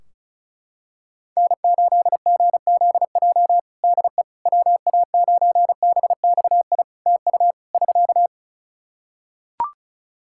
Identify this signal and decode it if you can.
Morse code, heard as a keyed tone, about 35 wpm, 700 Hz. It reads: N8GZJ DE WA9BXI TU <SK>
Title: K1USN SST QSO Practice 35wpm